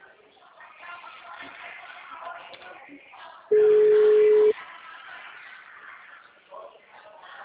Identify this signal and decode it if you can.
Ringback tone (Europe, 425 Hz), with speech.